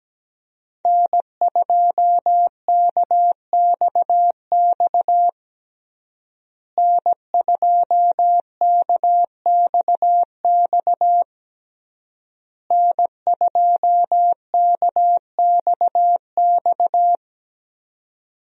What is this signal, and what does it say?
Morse code, heard as a keyed tone, about 17 wpm, 700 Hz. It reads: N2KXX N2KXX N2KXX
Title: USA Call Signs - 5 Characters - Repeated 3x - 17wpm